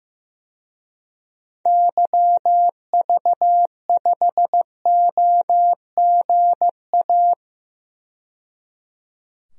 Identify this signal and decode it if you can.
Morse code, heard as a keyed tone, about 15 wpm, 700 Hz. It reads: YV5OGA
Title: DXCC-100 Call Signs - 6 Characters - 15wpm